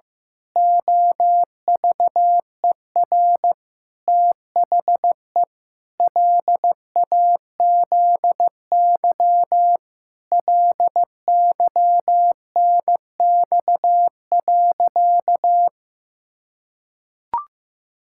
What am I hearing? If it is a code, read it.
Morse code, heard as a keyed tone, about 15 wpm, 700 Hz. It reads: OVER THE LAZY LYNX.
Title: Pangram Challenge v2 - 15wpm